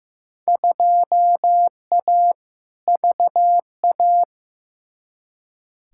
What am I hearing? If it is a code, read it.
Morse code, heard as a keyed tone, about 15 wpm, 700 Hz. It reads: 2A VA
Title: Field Day Exchanges - 15wpm